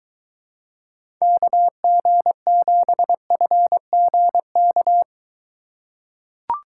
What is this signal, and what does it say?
Morse code, heard as a keyed tone, about 23 wpm, 700 Hz. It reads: KG7FGK